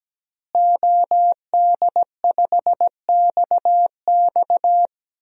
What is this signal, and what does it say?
Morse code, heard as a keyed tone, about 17 wpm, 700 Hz. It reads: OD5XX
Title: DXCC-100 Call Signs - 5 Characters - Repeated 2x - 17wpm